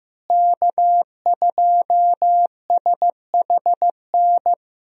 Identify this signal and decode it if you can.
Morse code, heard as a keyed tone, about 15 wpm, 700 Hz. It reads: K2SHN
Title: USA Call Signs - 5 Characters - Repeated 2x - 15wpm